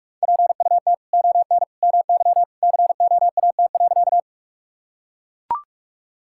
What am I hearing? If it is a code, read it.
Morse code, heard as a keyed tone, about 42 wpm, 700 Hz. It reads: PUT ON MY COAT.
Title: Sentences from Top 1600 Words 45wpm